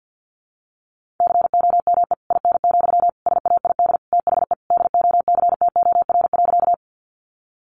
Morse code, heard as a keyed tone, about 50 wpm, 700 Hz. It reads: COME IN, SAID THE DOCTOR.
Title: Sentences from Top 1500 Words 50wpm